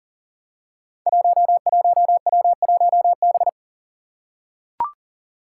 Morse code, heard as a keyed tone, about 41 wpm, 700 Hz. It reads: 11W1B
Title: Five Challenging Letter-Number Combinations 40wpm